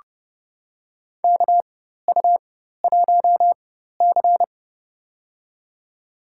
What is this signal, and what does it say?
Morse code, heard as a keyed tone, about 30 wpm, spaced as for 14 wpm, 700 Hz. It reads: KU1C
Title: USA Call Signs - 4x Character Spacing - 30wpm